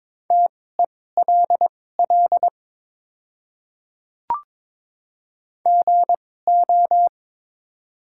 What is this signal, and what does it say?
Morse code, heard as a keyed tone, about 22 wpm, spaced as for 16 wpm, 700 Hz. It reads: TELL GO
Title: 2x Character Spacing - Top 200 Words - 22wpm